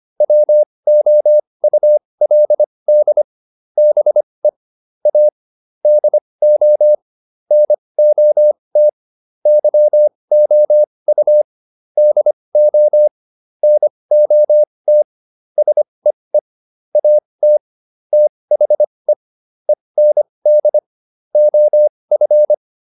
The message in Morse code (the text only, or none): WOULD BE A DO NOT YOU DO NOT SEE AT THE END OF